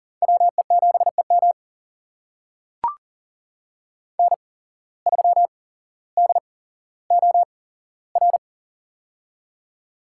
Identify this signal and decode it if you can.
Morse code, heard as a keyed tone, about 40 wpm, spaced as for 17 wpm, 700 Hz. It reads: WE7EM N3DOR
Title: USA Call Signs - 5 Characters - 8x Character Spacing - 40wpm